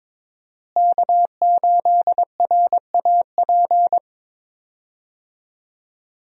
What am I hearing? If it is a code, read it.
Morse code, heard as a keyed tone, about 22 wpm, 700 Hz. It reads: K8RAP
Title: USA Call Signs - 5 Characters - Repeated 3x - 22wpm